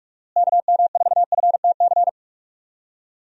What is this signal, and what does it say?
Morse code, heard as a keyed tone, about 46 wpm, 700 Hz. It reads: KM4FTC
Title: USA Call Signs - 6 Characters - Repeated 3x - 45wpm